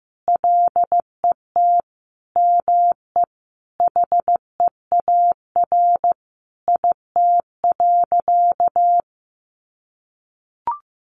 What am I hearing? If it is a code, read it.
Morse code, heard as a keyed tone, about 14 wpm, 700 Hz. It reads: LET ME HEAR IT.
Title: Sentences from Top 1500 Words 15wpm